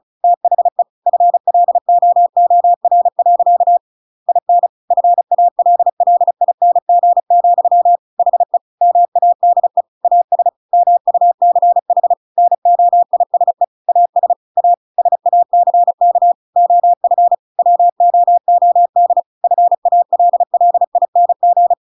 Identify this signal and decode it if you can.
Morse code, heard as a keyed tone, about 36 wpm, 700 Hz. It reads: THE FLOOR. IN FALLING, HE MADE AS MUCH NOISE AS A SACK OF WOOD FALLING
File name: Pinocchio / Pinocchio ch7 35 35 ik1hge